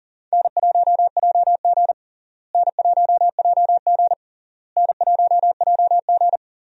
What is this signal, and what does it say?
Morse code, heard as a keyed tone, about 40 wpm, 700 Hz. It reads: N1JG N1JG N1JG
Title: USA Call Signs - 4 Characters - Repeated 3x - 40wpm